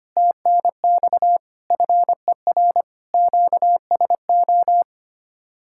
Morse code, heard as a keyed tone, about 31 wpm, 700 Hz. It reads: TNX FER QSO